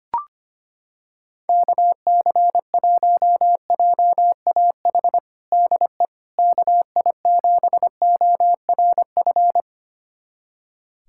Morse code, heard as a keyed tone, about 25 wpm, 700 Hz. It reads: KC1JAH DE KI7ORF
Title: USA Callsign DE Callsign 25wpm